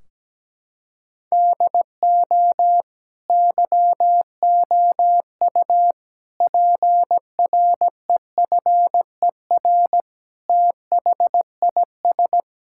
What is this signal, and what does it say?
Morse code, heard as a keyed tone, about 17 wpm, 700 Hz. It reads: DO YOU PREFER THIS
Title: Sentences from Top 1600 Words 17wpm